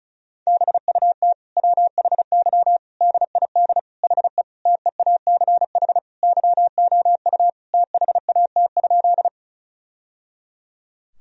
Morse code, heard as a keyed tone, about 36 wpm, 700 Hz. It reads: BUT WHY DID HE TEACH YOU THAT?
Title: Sentences from Top 1000 Words - Review Part 2 - 35wpm